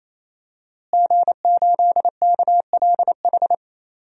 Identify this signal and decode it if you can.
Morse code, heard as a keyed tone, about 28 wpm, 700 Hz. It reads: G8KLH
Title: DXCC-100 Call Signs - 5 Characters - Repeated 3x - 28wpm